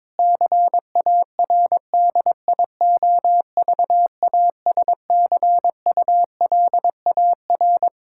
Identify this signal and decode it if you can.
Morse code, heard as a keyed tone, about 22 wpm, 700 Hz. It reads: CARDIOVASCULAR
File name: Difficult Sentences with Hints - 22wpm